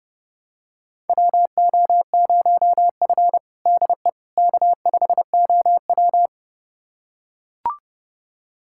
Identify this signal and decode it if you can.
Morse code, heard as a keyed tone, about 29 wpm, 700 Hz. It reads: WO0F DE K5OW
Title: USA Callsign DE Callsign - 4 Characters - 30wpm